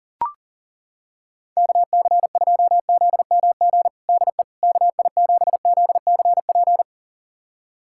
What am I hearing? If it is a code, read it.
Morse code, heard as a keyed tone, about 40 wpm, 700 Hz. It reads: KC2ZMG DE KI7ZCP
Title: USA Callsign DE Callsign - 6 Characters - 40wpm